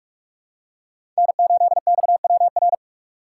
Morse code, heard as a keyed tone, about 46 wpm, 700 Hz. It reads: N8XWR
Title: USA Call Signs - 5 Characters - 4x Character Spacing - 45wpm